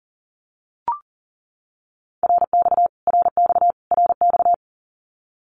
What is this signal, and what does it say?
Morse code, heard as a keyed tone, about 40 wpm, 700 Hz. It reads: RX RX RX
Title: Abbreviations Repeated 3x - 40wpm